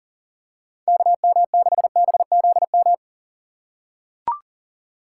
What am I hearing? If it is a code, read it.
Morse code, heard as a keyed tone, about 40 wpm, 700 Hz. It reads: KM6BZM
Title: USA Call Signs - 6 Characters - 2x Character Spacing - 40wpm